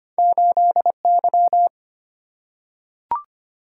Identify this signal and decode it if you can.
Morse code, heard as a keyed tone, about 25 wpm, 700 Hz. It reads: 8Y
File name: Two Letter-Number Combinations - 25wpm